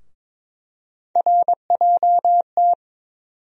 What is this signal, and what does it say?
Morse code, heard as a keyed tone, about 22 wpm, 700 Hz. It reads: RJT
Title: USA Call Sign Postfixes - 3 Letters 22wpm